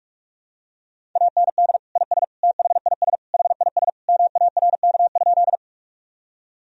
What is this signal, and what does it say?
Morse code, heard as a keyed tone, about 44 wpm, 700 Hz. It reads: AND IS THIS HIS MARK?
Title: Sentences from Top 1000 Words - Review Part 1 - 45wpm